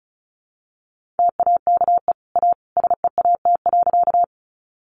Morse code, heard as a keyed tone, about 34 wpm, 700 Hz. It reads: TAKE A SEAT.
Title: Sentences from Top 1500 Words 35wpm